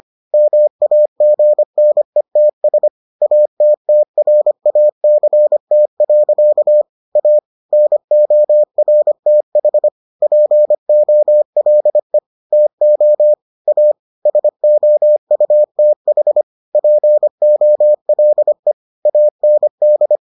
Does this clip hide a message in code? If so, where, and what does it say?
Yes: Morse code from 0.3 to 20.2 s, heard as a keyed tone, about 25 wpm, 600 Hz. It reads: MAGNETS ATTRACT. A NORTH POLE TO A SOUTH POLE AND